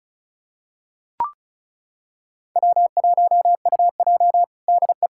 Morse code, heard as a keyed tone, about 34 wpm, 700 Hz. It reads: W1UJ DE
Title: K1USN SST QSO Practice 35wpm